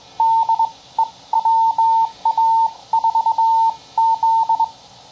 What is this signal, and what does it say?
Morse code, heard as a keyed tone, about 22 wpm, spaced as for 18 wpm, 900 Hz. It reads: DE WA4Z
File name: bug sample